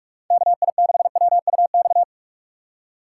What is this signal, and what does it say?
Morse code, heard as a keyed tone, about 44 wpm, 700 Hz. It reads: KI6WUX
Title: USA Call Signs - 6 Characters - 4x Character Spacing - 45wpm